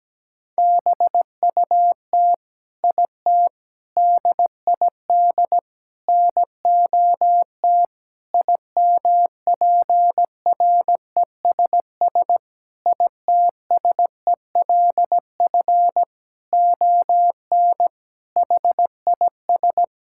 Morse code, heard as a keyed tone, about 17 wpm, 700 Hz. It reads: BUT IT DID NOT IMPRESS ITSELF ON HIS
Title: Sentences from Top 1600 Words 17wpm